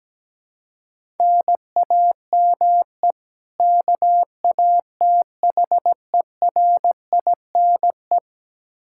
Morse code, heard as a keyed tone, about 17 wpm, 700 Hz. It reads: NAME KATHERINE
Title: Practice QSO - Exchange 1 - Beginning - 17wpm